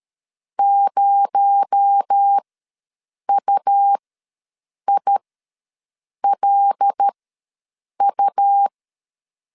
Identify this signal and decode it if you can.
Morse code, heard as a keyed tone, about 13 wpm, spaced as for 7 wpm, 800 Hz. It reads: 0UILU
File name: LEZIONI / 6 PKX0 / 6 PKX0 ws 12wpm 3448